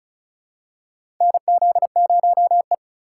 Morse code, heard as a keyed tone, about 36 wpm, 700 Hz. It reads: NZ0E
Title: USA Call Signs - 4 Characters - 2x Character Spacing - 35wpm